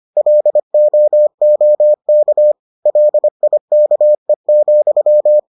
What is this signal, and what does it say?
Morse code, heard as a keyed tone, about 25 wpm, 600 Hz. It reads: LOOK LIKE,